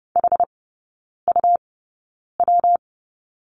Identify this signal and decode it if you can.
Morse code, heard as a keyed tone, about 30 wpm, spaced as for 9 wpm, 700 Hz. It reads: HUW
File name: Lesson 19e - Letter w - Three Characters 30wpm